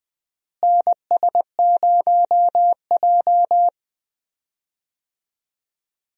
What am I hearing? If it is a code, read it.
Morse code, heard as a keyed tone, about 20 wpm, 700 Hz. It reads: NS0J
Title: USA Call Signs - 4 Characters - Repeated 2x - 20wpm